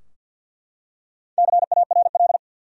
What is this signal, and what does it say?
Morse code, heard as a keyed tone, about 51 wpm, 700 Hz. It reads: CARL
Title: USA Common Names - Max 4 Letters - 50wpm